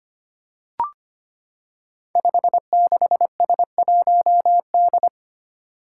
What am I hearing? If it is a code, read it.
Morse code, heard as a keyed tone, about 25 wpm, 700 Hz. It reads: 56S1D
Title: Five Challenging Letter-Number Combinations 25wpm